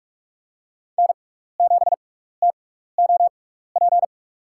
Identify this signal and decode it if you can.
Morse code, heard as a keyed tone, about 45 wpm, spaced as for 15 wpm, 700 Hz. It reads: N7TOP
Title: USA Call Signs - 5 Characters - 6x Character Spacing - 45wpm